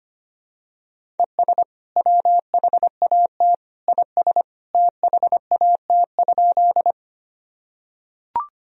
Morse code, heard as a keyed tone, about 25 wpm, 700 Hz. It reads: ES WHAT IS THAT?
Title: Sentences from Top 100 Words and Abbreviations 25wpm